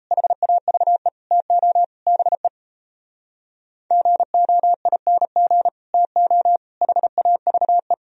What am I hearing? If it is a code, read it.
Morse code, heard as a keyed tone, about 38 wpm, 700 Hz. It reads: HAVE TO BE GOING TO HAVE